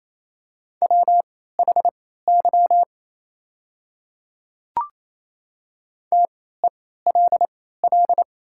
Morse code, heard as a keyed tone, about 27 wpm, spaced as for 16 wpm, 700 Hz. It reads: WHY TELL
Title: 3x Character Spacing - Top 500 Words - 28wpm